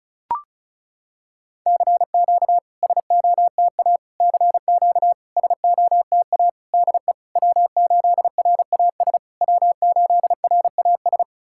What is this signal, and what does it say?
Morse code, heard as a keyed tone, about 36 wpm, 690 Hz. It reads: CQ SOTA CQ SOTA DE W8RAS W8RAS
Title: SOTA QSO Practice 35wpm